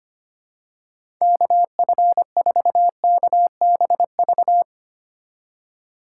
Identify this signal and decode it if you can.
Morse code, heard as a keyed tone, about 25 wpm, 700 Hz. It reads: KF4KBV
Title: USA Call Signs - 6 Characters - 4x Character Spacing - 25wpm